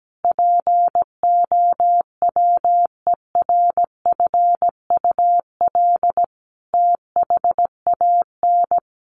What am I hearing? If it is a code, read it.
Morse code, heard as a keyed tone, about 17 wpm, 700 Hz. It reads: POWERFUL THAN
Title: Sentences from Top 1300 Words 17wpm